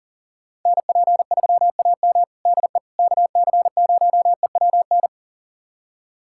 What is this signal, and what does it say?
Morse code, heard as a keyed tone, about 40 wpm, 700 Hz. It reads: NP3AM DE KC0EWN
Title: USA Callsign DE Callsign 40wpm